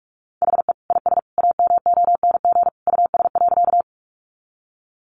Morse code, heard as a keyed tone, about 44 wpm, 700 Hz. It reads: HE IS AMONG US.
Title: Sentences from Top 500 Words 45wpm